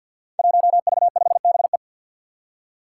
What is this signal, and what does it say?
Morse code, heard as a keyed tone, about 51 wpm, 700 Hz. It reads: 1V5BE
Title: Five Challenging Letter-Number Combinations 50wpm